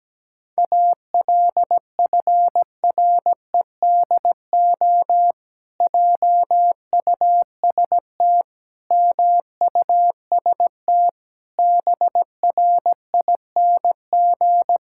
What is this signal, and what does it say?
Morse code, heard as a keyed tone, about 17 wpm, 700 Hz. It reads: ALFREDO JUST MUST BRING
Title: Pangram Challenge v2 - 17wpm